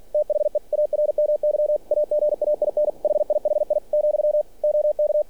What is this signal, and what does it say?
Morse code, heard as a keyed tone, about 46 wpm, 600 Hz. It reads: THE ARMY AGAIN HIHI , OK
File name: archived / CRSnet